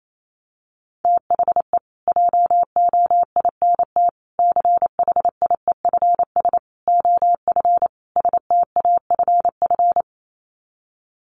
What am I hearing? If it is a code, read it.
Morse code, heard as a keyed tone, about 28 wpm, 700 Hz. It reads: THE JOINT CHIEFS OF STAFF
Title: Sets of 5 Words Encore2 28wpm